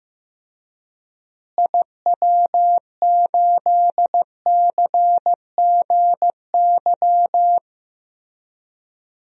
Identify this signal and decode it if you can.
Morse code, heard as a keyed tone, about 15 wpm, 700 Hz. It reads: IW8CGY
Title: DXCC-100 Call Signs - 6 Characters - Repeated 3x - 15wpm